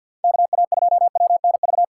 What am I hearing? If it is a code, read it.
Morse code, heard as a keyed tone, about 52 wpm, 700 Hz. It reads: KA2WNV
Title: USA Call Signs - 3x Character Spacing - 50wpm